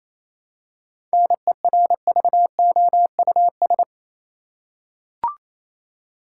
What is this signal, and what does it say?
Morse code, heard as a keyed tone, about 28 wpm, 700 Hz. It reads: NERVOUS
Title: Seven Letter Words - 28wpm